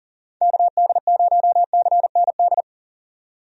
Morse code, heard as a keyed tone, about 40 wpm, 700 Hz. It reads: KD0CND
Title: USA Call Signs - 6 Characters - 40wpm